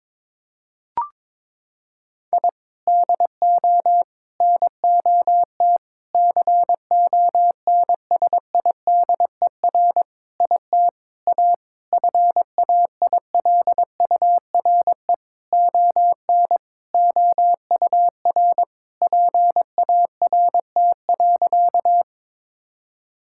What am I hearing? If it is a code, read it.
Morse code, heard as a keyed tone, about 22 wpm, 700 Hz. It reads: I DO NOT CONSIDER IT A FAILURE ON OUR PART.
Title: Sentences from Top 1000 Words - Review Part 6 - 22wpm